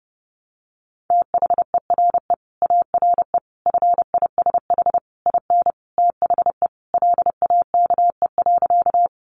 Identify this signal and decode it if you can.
Morse code, heard as a keyed tone, about 30 wpm, 700 Hz. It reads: THERE ARE FISH IN THE LAKE.
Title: Sentences from Top 1100 Words 30wpm